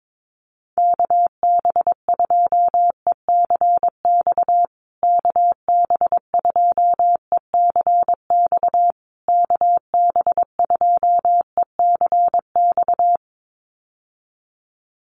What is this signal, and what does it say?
Morse code, heard as a keyed tone, about 22 wpm, 700 Hz. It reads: KB2ECX KB2ECX KB2ECX
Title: USA Call Signs Repeated 3x 22wpm